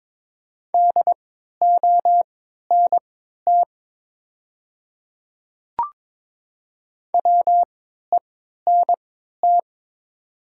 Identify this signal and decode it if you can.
Morse code, heard as a keyed tone, about 22 wpm, spaced as for 12 wpm, 690 Hz. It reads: DONT WENT
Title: 3x Character Spacing - Top 300 Words - 22wpm